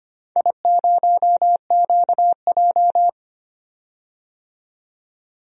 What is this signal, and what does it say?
Morse code, heard as a keyed tone, about 25 wpm, 700 Hz. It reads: I0QJ
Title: DXCC-100 Call Signs - 4 Characters - 25wpm